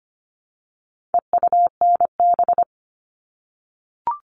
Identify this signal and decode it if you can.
Morse code, heard as a keyed tone, about 25 wpm, 700 Hz. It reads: EUNB
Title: All Cut Numbers - 4-Digit 25wpm